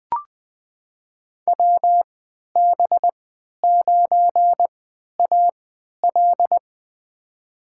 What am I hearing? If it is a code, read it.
Morse code, heard as a keyed tone, about 20 wpm, spaced as for 11 wpm, 700 Hz. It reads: WB9AL
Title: USA Call Signs - 3x Character Spacing - 20wpm